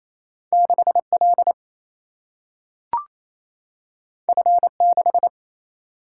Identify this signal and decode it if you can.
Morse code, heard as a keyed tone, about 28 wpm, 700 Hz. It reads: 6L F6
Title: Single-Letter Single-Number 28wpm